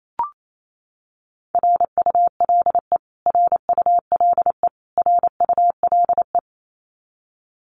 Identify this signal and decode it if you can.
Morse code, heard as a keyed tone, about 28 wpm, 700 Hz. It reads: RULE RULE RULE
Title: Top 500 Words Repeated 28wpm